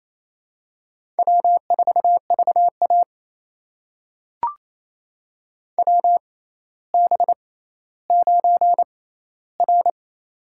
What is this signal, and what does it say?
Morse code, heard as a keyed tone, about 27 wpm, spaced as for 14 wpm, 700 Hz. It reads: W4VA WB9R
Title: USA Call Signs - 4 Characters - 6x Character Spacing - 28wpm